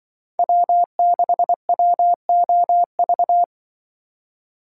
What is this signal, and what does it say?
Morse code, heard as a keyed tone, about 24 wpm, 700 Hz. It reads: W6WOV
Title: Speed Racing - USA 1x3 Call Signs 20wpm